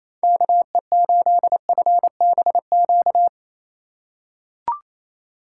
Morse code, heard as a keyed tone, about 28 wpm, 700 Hz. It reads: KE8FBQ
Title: USA Call Signs - 6 Characters - 4x Character Spacing - 28wpm